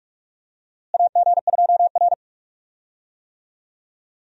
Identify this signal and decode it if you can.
Morse code, heard as a keyed tone, about 41 wpm, 700 Hz. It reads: AG2R
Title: USA Call Signs - 4 Characters - Repeated 3x - 45wpm